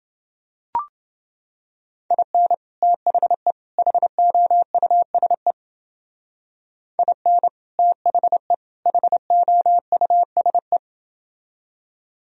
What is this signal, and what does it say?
Morse code, heard as a keyed tone, about 29 wpm, 700 Hz. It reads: IN THE HOUSE IN THE HOUSE